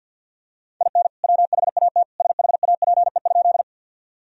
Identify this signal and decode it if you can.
Morse code, heard as a keyed tone, about 48 wpm, 700 Hz. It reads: IN WHAT SHAPE?
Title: Sentences from Top 1000 Words - Review Part 3 - 50wpm